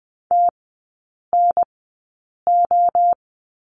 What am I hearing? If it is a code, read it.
Morse code, heard as a keyed tone, about 20 wpm, spaced as for 8 wpm, 700 Hz. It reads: TNO
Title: 035 Lesson 7e - Letter s - Three Characters 20wpm